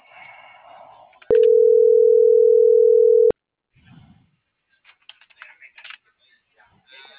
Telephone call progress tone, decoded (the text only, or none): ringback tone